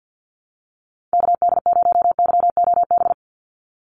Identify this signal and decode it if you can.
Morse code, heard as a keyed tone, about 50 wpm, 700 Hz. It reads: KD0YOB
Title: NA SOTA Activator Callsigns 50wpm